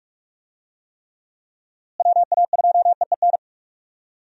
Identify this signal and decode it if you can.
Morse code, heard as a keyed tone, about 42 wpm, 700 Hz. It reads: WA2EEN
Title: USA Call Signs - 6 Characters - Repeated 3x - 45wpm